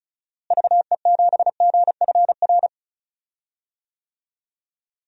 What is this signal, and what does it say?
Morse code, heard as a keyed tone, about 34 wpm, 700 Hz. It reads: VE7GFR
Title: DXCC-100 Call Signs - 6 Characters - Repeated 3x - 35wpm